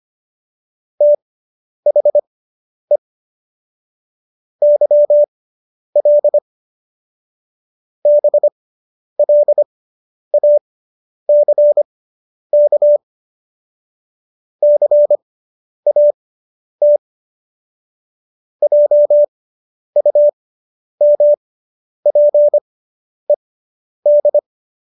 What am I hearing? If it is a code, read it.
Morse code, heard as a keyed tone, about 25 wpm, spaced as for 10 wpm, 590 Hz. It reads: THE YL BLACK CAT JUMPED